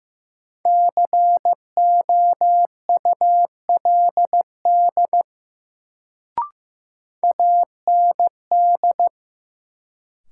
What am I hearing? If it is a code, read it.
Morse code, heard as a keyed tone, about 15 wpm, 700 Hz. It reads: COULD AND
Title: Sentences from Top 100 Words - New Word Drill - 15wpm